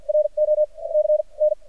Morse code, heard as a keyed tone, about 50 wpm, spaced as for 41 wpm, 610 Hz. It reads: RO C N
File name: archived / CRSnet